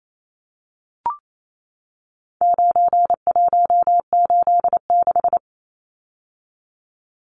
Morse code, heard as a keyed tone, about 28 wpm, 700 Hz. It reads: 9186
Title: Numbers - 4-Digit 28wpm